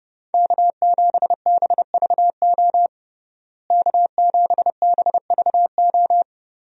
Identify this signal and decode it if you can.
Morse code, heard as a keyed tone, about 30 wpm, 700 Hz. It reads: K7BVO K7BVO
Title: USA Call Signs - 5 Characters - Repeated 2x - 30wpm